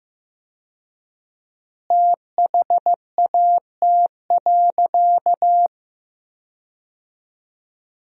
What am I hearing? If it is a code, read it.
Morse code, heard as a keyed tone, about 15 wpm, 700 Hz. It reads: THAT.
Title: 4x Word-Spacing - Sentences from Top 300 Words - 15wpm